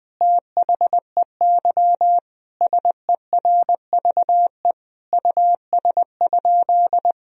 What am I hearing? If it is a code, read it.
Morse code, heard as a keyed tone, about 20 wpm, 700 Hz. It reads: THEY SERVE US?
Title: Sentences from Top 1000 Words - Review Part 2 - 20wpm